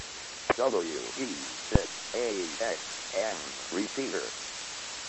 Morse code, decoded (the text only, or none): none